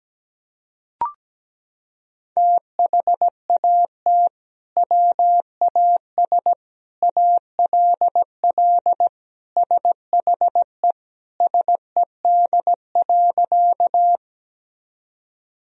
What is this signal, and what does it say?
Morse code, heard as a keyed tone, about 17 wpm, 700 Hz. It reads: THAT WAS ALL SHE SED.
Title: Sentences from Top 100 Words and Abbreviations 17wpm